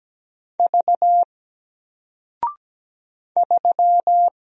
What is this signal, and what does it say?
Morse code, heard as a keyed tone, about 17 wpm, 700 Hz. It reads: V 3
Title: Single Letter-Number 17wpm